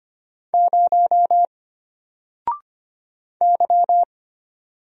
Morse code, heard as a keyed tone, about 25 wpm, 700 Hz. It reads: 0 Y
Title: Single Letter-Number Alt 25wpm